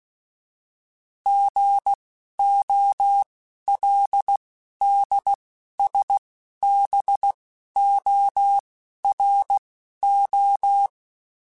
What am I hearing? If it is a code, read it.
Morse code, heard as a keyed tone, about 16 wpm, spaced as for 11 wpm, 800 Hz. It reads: GOLDSBORO
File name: fromSMs / FromSMs mp3s / Cities@10